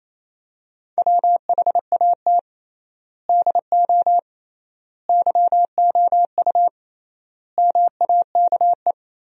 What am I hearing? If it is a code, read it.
Morse code, heard as a keyed tone, about 27 wpm, 700 Hz. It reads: WHAT DO YOU MAKE
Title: 3x Word-Spacing - Sentences from Top 100 Words - 28wpm